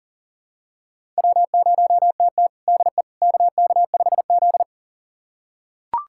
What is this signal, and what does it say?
Morse code, heard as a keyed tone, about 40 wpm, 700 Hz. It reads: W0TT DE KK5Z
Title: USA Callsign DE Callsign - 4 Characters - 40wpm